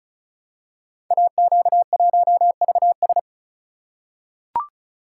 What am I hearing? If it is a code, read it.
Morse code, heard as a keyed tone, about 36 wpm, 700 Hz. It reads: AQ1VS
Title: Code Groups - 5 Letter:Numbers - 35wpm